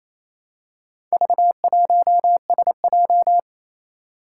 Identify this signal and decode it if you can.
Morse code, heard as a keyed tone, about 29 wpm, 700 Hz. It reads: V1SJ
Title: Four Challenging Letter-Number Combinations 28wpm